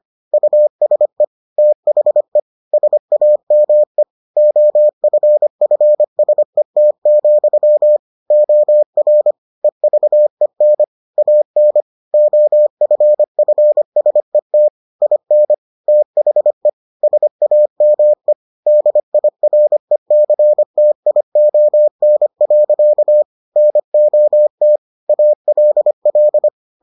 Morse code, heard as a keyed tone, about 25 wpm, 600 Hz. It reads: USE THE SAME OFFSET, OR EVEN AN OFFSET IN THE SAME DIRECTION. NOT ALL